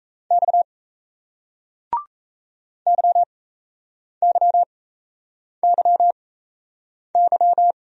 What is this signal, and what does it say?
Morse code, heard as a keyed tone, about 39 wpm, spaced as for 8 wpm, 700 Hz. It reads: X YYYY